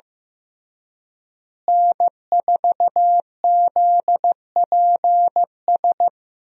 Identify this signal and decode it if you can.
Morse code, heard as a keyed tone, about 15 wpm, 700 Hz. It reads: N4ZPS
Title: USA Call Signs - 5 Characters - Repeated 3x - 15wpm